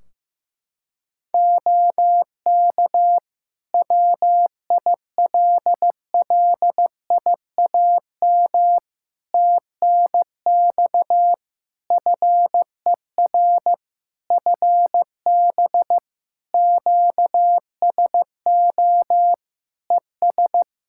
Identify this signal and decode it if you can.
Morse code, heard as a keyed tone, about 15 wpm, 700 Hz. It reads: OK WILLIAM TNX FER FB QSO ES
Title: Practice QSO - Exchange 4 - Ending - 15wpm